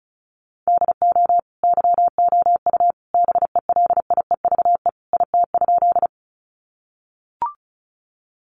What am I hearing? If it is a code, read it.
Morse code, heard as a keyed tone, about 34 wpm, 700 Hz. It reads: DO YOU BELIEVE IT?
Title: Sentences from Top 1000 Words 35wpm